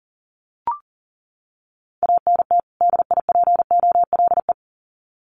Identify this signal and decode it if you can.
Morse code, heard as a keyed tone, about 40 wpm, 700 Hz. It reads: ANT DIPOLE
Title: Common QSO Phrases 40wpm